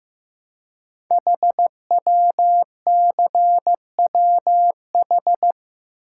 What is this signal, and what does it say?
Morse code, heard as a keyed tone, about 15 wpm, 700 Hz. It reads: HWCWH
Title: Code Groups - 5 Letters - 15wpm